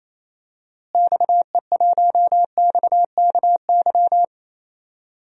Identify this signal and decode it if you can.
Morse code, heard as a keyed tone, about 28 wpm, 700 Hz. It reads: XE1XKY
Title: DXCC-100 Call Signs - 6 Characters - Repeated 3x - 28wpm